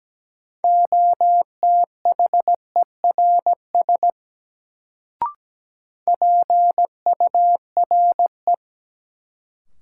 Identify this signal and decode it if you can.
Morse code, heard as a keyed tone, about 17 wpm, 700 Hz. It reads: OTHERS PURE
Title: Sentences from Top 1500 Words - New Word Drill Alt - 17wpm